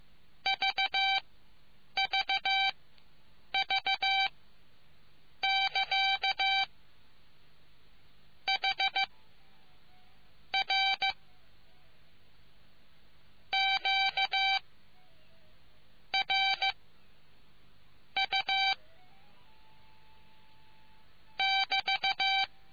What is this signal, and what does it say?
Morse code, heard as a keyed tone, about 15 wpm, spaced as for 7 wpm, 780 Hz. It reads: VVV<KA> H R Q R U <BT>